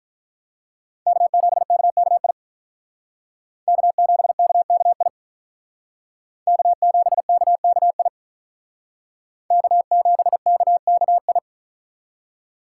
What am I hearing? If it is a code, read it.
Morse code, heard as a keyed tone, about 45 wpm, 700 Hz. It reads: K7KKI K7KKI K7KKI K7KKI